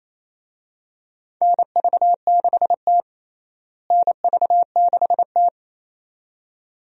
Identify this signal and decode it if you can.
Morse code, heard as a keyed tone, about 27 wpm, 700 Hz. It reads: NV6T NV6T
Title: USA Call Signs - 4 Characters - Repeated 2x - 28wpm